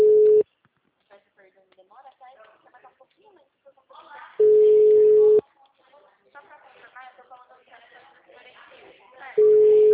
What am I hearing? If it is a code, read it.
Ringback tone (Europe, 425 Hz), with speech.